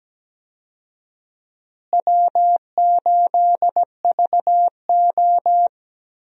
Morse code, heard as a keyed tone, about 17 wpm, 700 Hz. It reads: W8VO
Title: USA Call Signs - 4 Characters - Repeated 2x - 17wpm